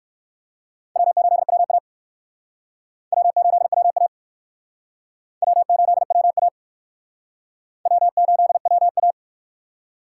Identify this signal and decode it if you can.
Morse code, heard as a keyed tone, about 56 wpm, 700 Hz. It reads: W8WA W8WA W8WA W8WA